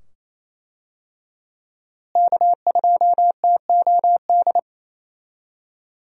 Morse code, heard as a keyed tone, about 27 wpm, 700 Hz. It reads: K2TOD
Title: USA Call Signs - 5 Characters - Repeated 2x - 28wpm